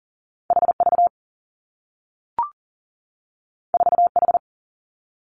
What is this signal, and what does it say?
Morse code, heard as a keyed tone, about 42 wpm, 700 Hz. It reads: H V 4 H
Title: Two Challenging Letter-Number Combinations 40wpm